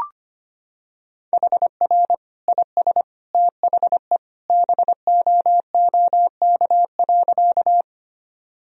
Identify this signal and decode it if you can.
Morse code, heard as a keyed tone, about 25 wpm, 700 Hz. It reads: HR IS THE BOOK.
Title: Sentences from Top 300 Words and Abbreviations 25wpm